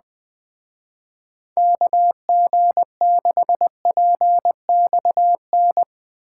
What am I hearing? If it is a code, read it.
Morse code, heard as a keyed tone, about 22 wpm, 700 Hz. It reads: KG6PXN
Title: Speed Racing - USA 2x3 Call Signs 20wpm